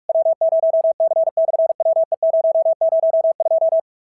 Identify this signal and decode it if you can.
Morse code, heard as a keyed tone, about 44 wpm, 650 Hz. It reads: W0C/WE002
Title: SOTA QSO Practice 45wpm